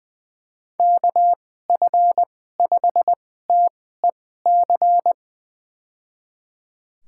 Morse code, heard as a keyed tone, about 20 wpm, spaced as for 14 wpm, 700 Hz. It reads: KF5TEC
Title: USA Call Signs - 6 Characters - 2x Character Spacing - 20wpm